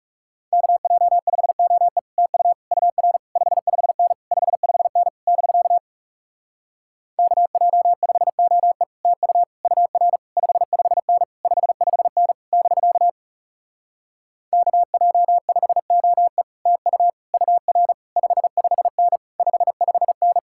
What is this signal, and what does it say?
Morse code, heard as a keyed tone, about 40 wpm, 700 Hz. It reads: KJ5OE TU UR 55N 55N <BK> KJ5OE TU UR 55N 55N <BK> KJ5OE TU UR 55N 55N